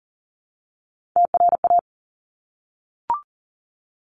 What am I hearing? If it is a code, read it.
Morse code, heard as a keyed tone, about 39 wpm, 700 Hz. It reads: TRA
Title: Top Three Letter Combinations in Words 40wpm